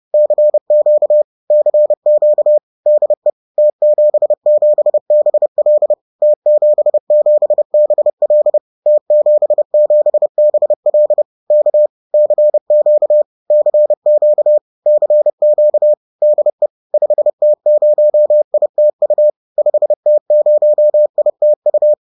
Morse code, heard as a keyed tone, about 30 wpm, 600 Hz. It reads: CQ CQ DE T77BL T77BL T77BL K CQ CQ CQ DE 5T0ITU 5T0ITU